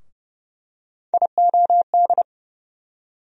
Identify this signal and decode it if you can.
Morse code, heard as a keyed tone, about 30 wpm, 700 Hz. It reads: IOD
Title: USA Call Sign Postfixes - 3 Letters 30wpm